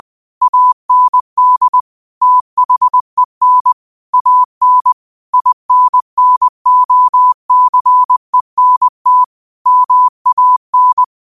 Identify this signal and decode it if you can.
Morse code, heard as a keyed tone, about 20 wpm, 1000 Hz. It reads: AND THEN AN INNOCENT MAN
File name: qotdfiles / 20wpm